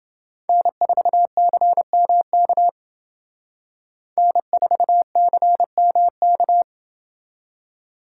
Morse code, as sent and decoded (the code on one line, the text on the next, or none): -. ....- -.-. -- -.- / -. ....- -.-. -- -.-
N4CMK N4CMK